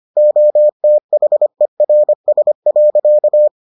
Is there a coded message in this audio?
Yes: Morse code from 0.2 to 3.5 s, heard as a keyed tone, about 25 wpm, 600 Hz. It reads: OTHERS.